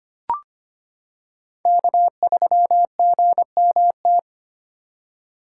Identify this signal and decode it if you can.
Morse code, heard as a keyed tone, about 25 wpm, 700 Hz. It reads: K3GMT
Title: K1USN SST QSO Practice - 25wpm